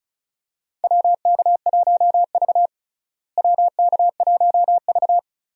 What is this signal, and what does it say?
Morse code, heard as a keyed tone, about 36 wpm, 700 Hz. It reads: WK1V WK1V
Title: USA Call Signs - 4 Characters - Repeated 2x - 35wpm